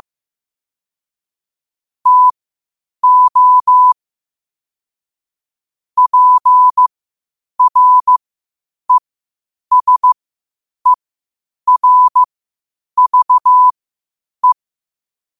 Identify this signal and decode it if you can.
Morse code, heard as a keyed tone, about 15 wpm, spaced as for 8 wpm, 1000 Hz. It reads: TO PRESERVE
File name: qotdfiles / 8wpm